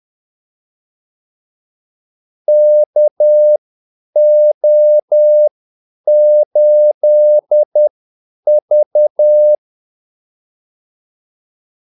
Morse code, heard as a keyed tone, about 10 wpm, spaced as for 8 wpm, 600 Hz. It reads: KO8V
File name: Callsigns-10-8wpm